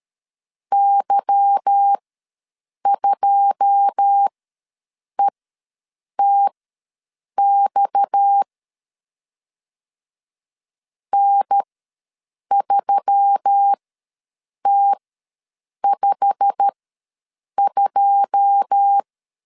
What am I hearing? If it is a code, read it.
Morse code, heard as a keyed tone, about 12 wpm, spaced as for 7 wpm, 800 Hz. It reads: Y2ETX N3T52
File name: LEZIONI / 9 2867 ws 12wpm 3431